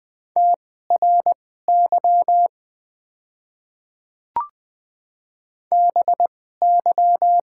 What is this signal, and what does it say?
Morse code, heard as a keyed tone, about 20 wpm, spaced as for 14 wpm, 700 Hz. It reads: TRY BY
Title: 2x Character Spacing - Top 200 Words - 20wpm